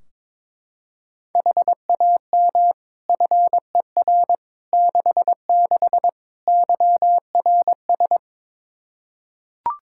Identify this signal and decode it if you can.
Morse code, heard as a keyed tone, about 21 wpm, 700 Hz. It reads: HAM FER 66 YRS
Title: Practice QSO - Exchange 3 - More Info - 22wpm